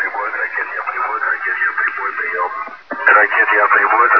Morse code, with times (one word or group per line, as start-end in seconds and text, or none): none